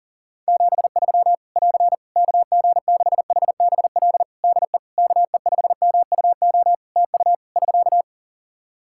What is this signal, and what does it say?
Morse code, heard as a keyed tone, about 40 wpm, 700 Hz. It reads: 73 <AR> KG6HBL DE KE5MUO TU <SK>
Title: Practice QSO - Exchange 4 - Ending - 40wpm